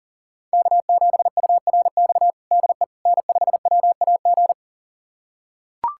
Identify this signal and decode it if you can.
Morse code, heard as a keyed tone, about 40 wpm, 700 Hz. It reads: K7URX DE N5WAG
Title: USA Callsign DE Callsign - 5 Characters - 40wpm